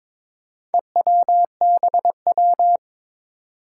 Morse code, heard as a keyed tone, about 22 wpm, 700 Hz. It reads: EWBW
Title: Four Challenging Letter-Number Combinations 22wpm